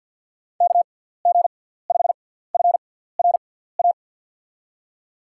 Morse code, heard as a keyed tone, about 52 wpm, spaced as for 17 wpm, 700 Hz. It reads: KG5FRA
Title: USA Call Signs - 6 Characters - 6x Character Spacing - 50wpm